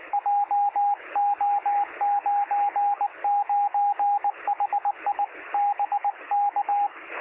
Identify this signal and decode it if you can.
Morse code, heard as a keyed tone, about 20 wpm, 810 Hz, with radio noise. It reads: JO99HI BK
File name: sm7lcb / 2057z sm0fzh 23cm